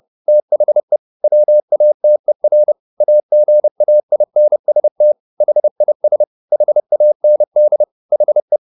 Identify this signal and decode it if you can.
Morse code, heard as a keyed tone, about 30 wpm, 600 Hz. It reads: THE WATER AGAINST HIS HAND HE